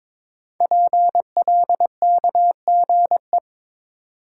Morse code, heard as a keyed tone, about 22 wpm, 700 Hz. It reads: PLKGE
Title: Code Groups - 5 Letters - 22wpm